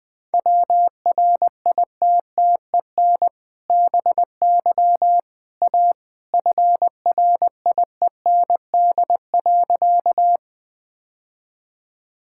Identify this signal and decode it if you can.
Morse code, heard as a keyed tone, about 19 wpm, 700 Hz. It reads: WRITTEN BY A FRIEND.
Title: Sentences with 2-Word Phrases - Spoken Beginning - 20wpm